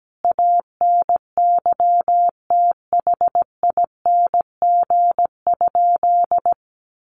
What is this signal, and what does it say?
Morse code, heard as a keyed tone, about 17 wpm, 700 Hz. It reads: ANYTHING?
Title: Sentences from Top 1000 Words 17wpm